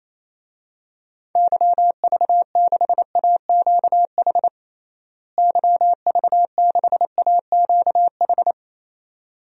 Morse code, heard as a keyed tone, about 28 wpm, 700 Hz. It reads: YV6AQH YV6AQH
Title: DXCC-100 Call Signs - 6 Characters - Repeated 2x - 28wpm